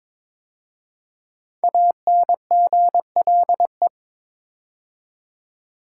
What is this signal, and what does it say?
Morse code, heard as a keyed tone, about 22 wpm, 700 Hz. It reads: ANGLE
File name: 5 Letter Words - First 3 Letters Given - 22wpm